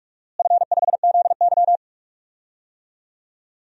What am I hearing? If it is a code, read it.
Morse code, heard as a keyed tone, about 42 wpm, 700 Hz. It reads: F5ZY
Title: DXCC-100 Call Signs - 4 Characters - Repeated 2x - 45wpm